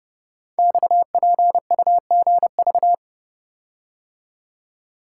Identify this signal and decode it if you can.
Morse code, heard as a keyed tone, about 30 wpm, 700 Hz. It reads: XPUGV
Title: Code Groups - 5 Letters - 30wpm